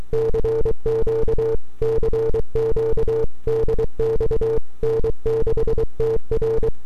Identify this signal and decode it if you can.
Morse code, heard as a keyed tone, about 23 wpm, 450 Hz. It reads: CQ CQ DX N6TR
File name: n6tr / 12Oct2009-160